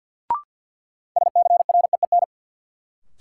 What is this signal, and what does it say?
Morse code, heard as a keyed tone, about 53 wpm, 700 Hz. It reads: SCREEN
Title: Sentences from Top 1000 Words - Review Part 6 - New Word Drill - 50wpm